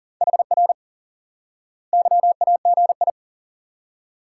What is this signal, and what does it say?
Morse code, heard as a keyed tone, about 40 wpm, 700 Hz. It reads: HR YAGI
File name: IQE Course 3 - Lesson 89 - YAGI - Two Elements - 40wpm